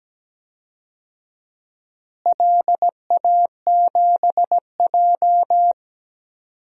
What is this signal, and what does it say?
Morse code, heard as a keyed tone, about 17 wpm, 700 Hz. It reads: LA7J
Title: DXCC-100 Call Signs - 4 Characters - Repeated 3x - 17wpm